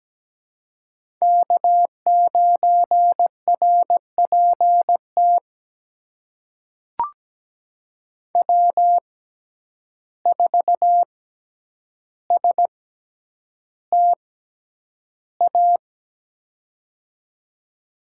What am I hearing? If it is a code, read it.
Morse code, heard as a keyed tone, about 17 wpm, spaced as for 9 wpm, 700 Hz. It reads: K9RPT W4STA
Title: USA Call Signs - 5 Characters - 6x Character Spacing - 17wpm